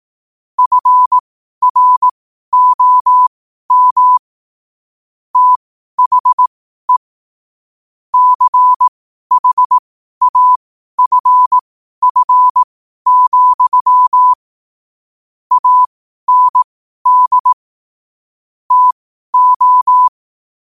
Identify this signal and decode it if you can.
Morse code, heard as a keyed tone, about 18 wpm, spaced as for 12 wpm, 1000 Hz. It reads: FROM THE CHAFF, AND TO
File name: qotdfiles / 12wpm